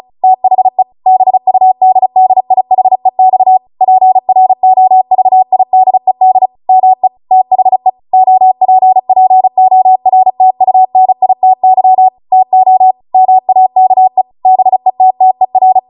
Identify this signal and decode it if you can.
Morse code, heard as a keyed tone, about 36 wpm, 750 Hz. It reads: THE BUDDIHEX PROVIDED ME THE OPPORTUNITY TO MAKE BETTER